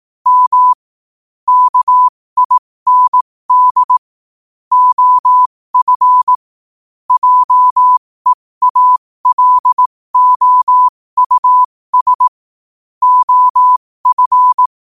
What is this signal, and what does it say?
Morse code, heard as a keyed tone, about 18 wpm, spaced as for 15 wpm, 1000 Hz. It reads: M KIND OF JEALOUS OF
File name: qotdfiles / 15wpm